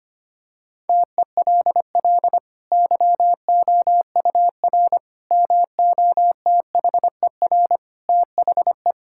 Morse code, heard as a keyed tone, about 25 wpm, 700 Hz. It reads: TELL YOUR MOTHER THE
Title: Sentences from Top 1600 Words 25wpm